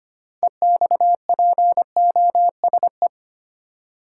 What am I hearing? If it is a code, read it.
Morse code, heard as a keyed tone, about 25 wpm, 700 Hz. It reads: EXPOSE
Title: Six Letter Words - 25wpm